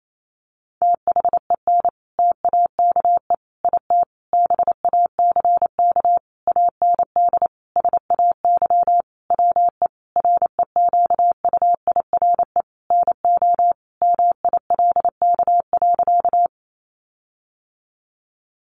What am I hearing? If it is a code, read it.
Morse code, heard as a keyed tone, about 28 wpm, 700 Hz. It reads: THEN TAKE IT BACK AND SAY WE REQUIRE NO MILK.
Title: Sentences from Top 1500 Words 28wpm